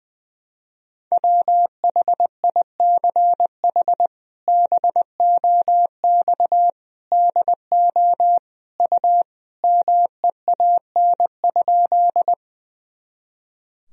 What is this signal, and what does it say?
Morse code, heard as a keyed tone, about 20 wpm, 700 Hz. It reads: WHICH BOX DO U MEAN?
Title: Sentences from Top 500 Words and Abbreviations 20wpm